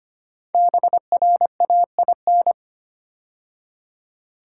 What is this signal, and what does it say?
Morse code, heard as a keyed tone, about 25 wpm, 700 Hz. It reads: BRAIN
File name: 5 Letter Words - First 3 Letters Given - 25wpm